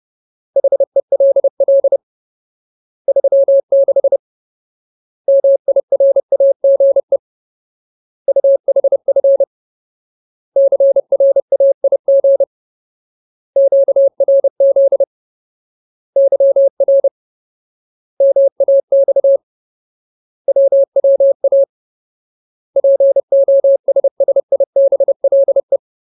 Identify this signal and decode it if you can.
Morse code, heard as a keyed tone, about 30 wpm, 550 Hz. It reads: HELL 36 MIRAGE UHF CRAIG QRZ YR MAX WWA POSSIBLE
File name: Ultimate Word List / 30WPM Ultimate